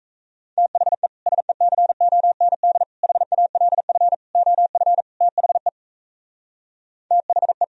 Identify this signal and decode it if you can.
Morse code, heard as a keyed tone, about 38 wpm, 700 Hz. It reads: THE SECOND HALF OF THE THE